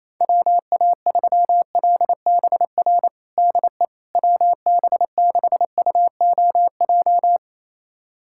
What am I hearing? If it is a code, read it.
Morse code, heard as a keyed tone, about 28 wpm, 700 Hz. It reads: WA3LBR DE WB6UOJ
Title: USA Callsign DE Callsign - 6 Characters - 28wpm